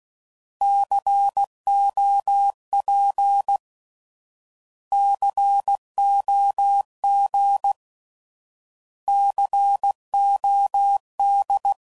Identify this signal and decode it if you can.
Morse code, heard as a keyed tone, about 16 wpm, 800 Hz. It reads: COP COG COD
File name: Prefixes / By First Letter / Cs